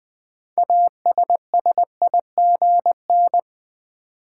Morse code, heard as a keyed tone, about 21 wpm, 700 Hz. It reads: ASSIGN
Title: 6 Letter Words - First 3 Letters Given - 20wpm